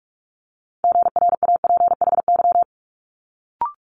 Morse code, heard as a keyed tone, about 46 wpm, 700 Hz. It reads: GRAPHY
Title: Multi-letter Phonemes Quick Fire 45wpm